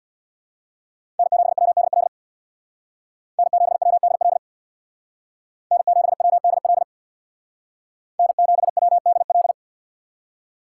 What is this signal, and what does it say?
Morse code, heard as a keyed tone, about 64 wpm, 700 Hz. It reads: N7WDL N7WDL N7WDL N7WDL